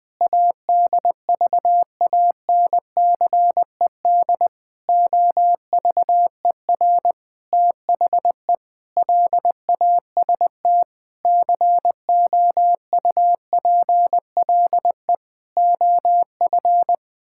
Morse code, heard as a keyed tone, about 21 wpm, 700 Hz. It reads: ADVANCED OVER THE LAST COUPLE OF
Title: Sentences with 6-Word Phrases - 20wpm